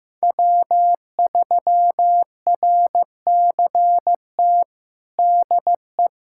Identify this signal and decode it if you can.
Morse code, heard as a keyed tone, about 15 wpm, 700 Hz. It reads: W3RCT DE
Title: USA Callsign DE Callsign - 5 Characters - 15wpm